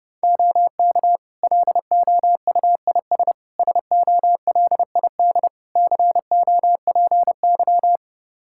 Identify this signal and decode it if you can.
Morse code, heard as a keyed tone, about 30 wpm, 700 Hz. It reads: OK LOUIS SOLID COPY
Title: Practice QSO - Exchange 3 - More Info - 30wpm